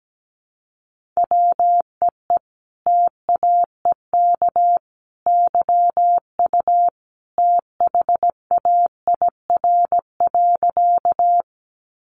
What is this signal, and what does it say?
Morse code, heard as a keyed tone, about 17 wpm, 700 Hz. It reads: WEE TAEK YU THAIR.
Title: A Sound Of Thunder 17wpm